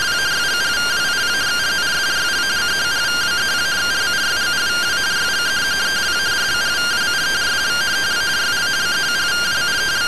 Two-tone teleprinter signal, about 170 Hz shift.